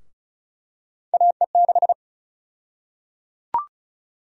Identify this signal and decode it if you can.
Morse code, heard as a keyed tone, about 36 wpm, 700 Hz. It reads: AE6
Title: USA Call Sign Prefixes - Repeated 2x - 35wpm